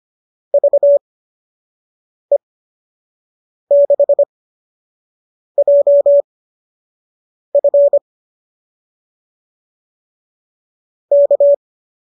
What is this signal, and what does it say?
Morse code, heard as a keyed tone, about 25 wpm, spaced as for 6 wpm, 570 Hz. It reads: VE6JF K
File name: qso002 06